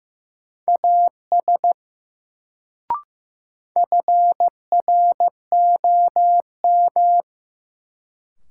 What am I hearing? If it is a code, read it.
Morse code, heard as a keyed tone, about 15 wpm, 700 Hz. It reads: AS FROM
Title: Sentences from Top 100 Words - New Word Drill - 15wpm